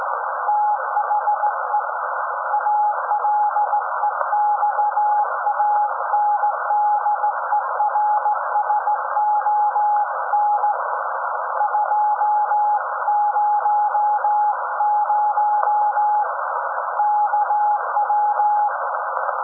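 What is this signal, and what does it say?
Morse code, heard as a keyed tone, about 17 wpm, 840 Hz. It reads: THE ALEXANDERSON 200 KW
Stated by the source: SAQ Grimeton Radio, Alexanderson 200 kW alternator on 17.2 kHz, received in Potsdam with a very strong signal on World Radio Day.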